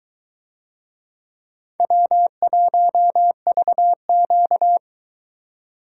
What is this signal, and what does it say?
Morse code, heard as a keyed tone, about 22 wpm, 700 Hz. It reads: W1VQ